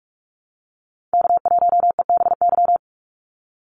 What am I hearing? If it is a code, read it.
Morse code, heard as a keyed tone, about 44 wpm, 700 Hz. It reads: K1EBY
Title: Call Signs Repeated 2x - 45wpm